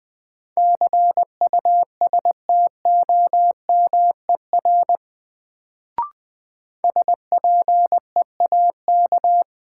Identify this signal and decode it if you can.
Morse code, heard as a keyed tone, about 20 wpm, 700 Hz. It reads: CUSTOMER SPEAK
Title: Sentences from Top 1100 Words - New Word Drill Alt - 20wpm